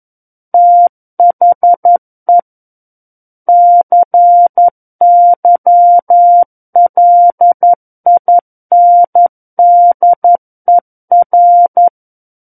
Morse code, heard as a keyed tone, about 11 wpm, 700 Hz. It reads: THE CYLINDER
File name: Book One / 02-Chapter 2 - 11 WPM